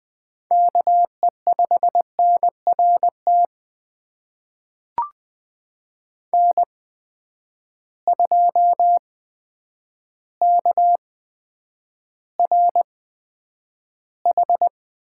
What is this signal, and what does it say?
Morse code, heard as a keyed tone, about 20 wpm, spaced as for 9 wpm, 700 Hz. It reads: KE5NRT N2KRH
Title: USA Call Signs - 8x Character Spacing - 20wpm